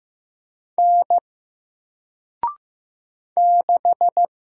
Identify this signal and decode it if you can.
Morse code, heard as a keyed tone, about 15 wpm, 700 Hz. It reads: N 6
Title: Single Letter-Number 15wpm